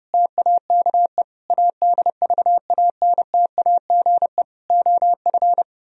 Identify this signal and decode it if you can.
Morse code, heard as a keyed tone, about 30 wpm, 700 Hz. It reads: TAKE ADVANTAGE OF